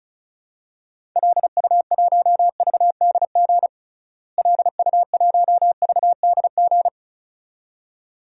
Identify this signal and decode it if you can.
Morse code, heard as a keyed tone, about 36 wpm, 700 Hz. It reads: LU1VDG LU1VDG
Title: DXCC-100 Call Signs - 6 Characters - Repeated 2x - 35wpm